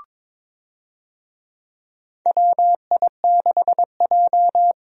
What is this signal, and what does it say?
Morse code, heard as a keyed tone, about 22 wpm, 700 Hz. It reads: WI6J
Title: USA Call Signs - 4 Characters - Repeated 2x - 22wpm